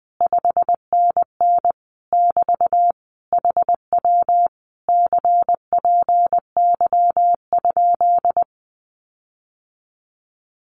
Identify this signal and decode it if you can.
Morse code, heard as a keyed tone, about 20 wpm, 700 Hz. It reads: 5NN <BT> HW CPY?
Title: Common QSO Phrases Encore 20wpm